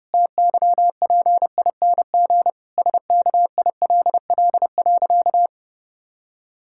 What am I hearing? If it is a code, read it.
Morse code, heard as a keyed tone, about 32 wpm, 700 Hz. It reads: TYPING SKILL.
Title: Pangram Challenge v2 - 30wpm